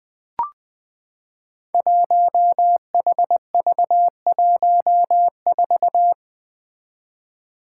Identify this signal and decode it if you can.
Morse code, heard as a keyed tone, about 20 wpm, 700 Hz. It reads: 1HV14
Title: Five Challenging Letter-Number Combinations 20wpm